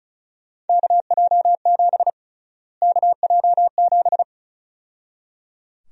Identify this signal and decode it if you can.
Morse code, heard as a keyed tone, about 36 wpm, 700 Hz. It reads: KJ7 KJ7
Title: USA Call Sign Prefixes - Repeated 2x - 35wpm